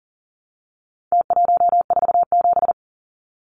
Morse code, heard as a keyed tone, about 40 wpm, 700 Hz. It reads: T147
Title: Numbers - Contest Style - 4-Digit 40wpm